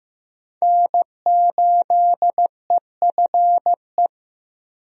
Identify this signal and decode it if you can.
Morse code, heard as a keyed tone, about 15 wpm, 700 Hz. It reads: N8EFE
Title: USA Call Signs - 5 Characters - Repeated 3x - 15wpm